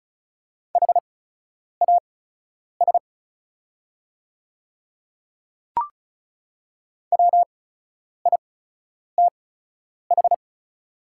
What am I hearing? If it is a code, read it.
Morse code, heard as a keyed tone, about 36 wpm, spaced as for 10 wpm, 700 Hz. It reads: HAS WITH
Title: 8x Character Spacing - Top 100 Words - 35wpm